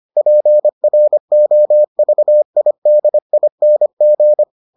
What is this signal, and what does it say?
Morse code, heard as a keyed tone, about 25 wpm, 600 Hz. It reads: PROVIDING